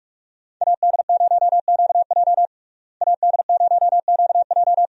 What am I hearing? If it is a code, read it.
Morse code, heard as a keyed tone, about 45 wpm, 700 Hz. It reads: AD0QJ AD0QJ
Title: USA Call Signs - 5 Characters - Repeated 2x - 45wpm